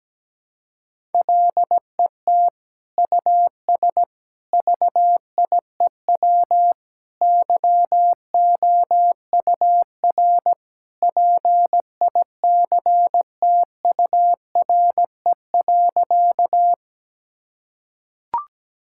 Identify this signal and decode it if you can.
Morse code, heard as a keyed tone, about 17 wpm, 700 Hz. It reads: LET US VIEW YOUR PICTURE.
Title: Sentences from Top 1000 Words - Review Part 6 - 17wpm